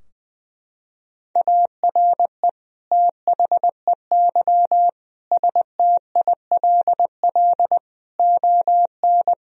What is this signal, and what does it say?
Morse code, heard as a keyed tone, about 21 wpm, 700 Hz. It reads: ARE THEY STILL ON
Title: Sentences from Top 1000 Words - Review Part 3 - 20wpm